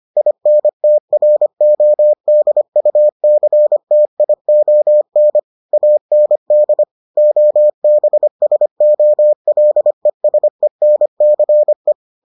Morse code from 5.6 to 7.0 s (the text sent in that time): AND